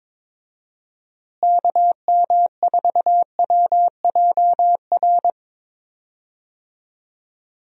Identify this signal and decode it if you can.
Morse code, heard as a keyed tone, about 22 wpm, 700 Hz. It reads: KM4WJR
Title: USA Call Signs - 6 Characters - Repeated 2x - 22wpm